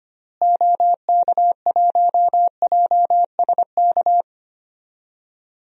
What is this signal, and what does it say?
Morse code, heard as a keyed tone, about 25 wpm, 700 Hz. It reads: OK1JSK
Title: DXCC-100 Call Signs - 6 Characters - Repeated 2x - 25wpm